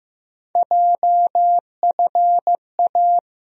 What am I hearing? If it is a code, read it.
Morse code, heard as a keyed tone, about 15 wpm, 700 Hz. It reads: JFA
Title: USA Call Sign Postfixes - 3 Letters 15wpm